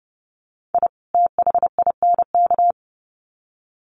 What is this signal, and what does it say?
Morse code, heard as a keyed tone, about 30 wpm, 700 Hz. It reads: I THINK
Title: Sets of 2 Words 30wpm